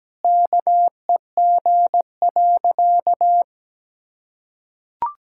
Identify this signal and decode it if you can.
Morse code, heard as a keyed tone, about 17 wpm, 700 Hz. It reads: KEG.
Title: Pangram Challenge v2 - 17wpm